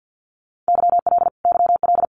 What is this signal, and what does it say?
Morse code, heard as a keyed tone, about 48 wpm, 700 Hz. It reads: YL YL
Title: Abbreviations Repeated 3x - 50wpm